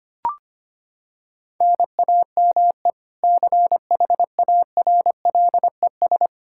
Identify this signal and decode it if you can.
Morse code, heard as a keyed tone, about 25 wpm, 700 Hz. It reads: NAME CHARLES
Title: Practice QSO - Exchange 1 - Beginning - 25wpm